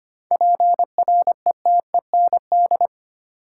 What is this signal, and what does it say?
Morse code, heard as a keyed tone, about 25 wpm, 700 Hz. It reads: PRETEND
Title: Seven Letter Words - 25wpm